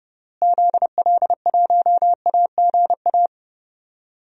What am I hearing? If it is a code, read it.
Morse code, heard as a keyed tone, about 32 wpm, 700 Hz. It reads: ZL1AGA
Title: DXCC-100 Call Signs - 6 Characters - Repeated 3x - 30wpm